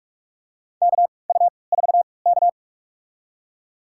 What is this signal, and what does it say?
Morse code, heard as a keyed tone, about 46 wpm, spaced as for 26 wpm, 700 Hz. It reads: KU4K
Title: USA Call Signs - 3x Character Spacing - 45wpm